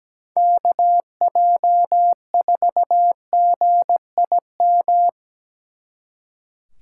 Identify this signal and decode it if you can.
Morse code, heard as a keyed tone, about 19 wpm, 700 Hz. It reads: KJ4GIM
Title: Speed Racing - USA Call Signs 17wpm